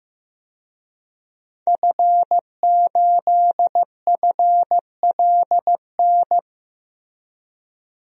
Morse code, heard as a keyed tone, about 15 wpm, 700 Hz. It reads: F8FLN
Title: DXCC-100 Call Signs - 5 Characters - Repeated 2x - 15wpm